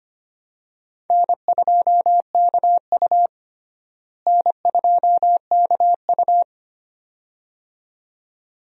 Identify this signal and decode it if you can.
Morse code, heard as a keyed tone, about 25 wpm, 700 Hz. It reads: N2KU N2KU
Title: USA Call Signs - 4 Characters - Repeated 2x - 25wpm